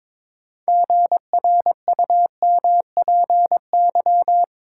Morse code, heard as a keyed tone, about 22 wpm, 700 Hz. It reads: GRUMPY
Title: Pangram Challenge v2 - 22wpm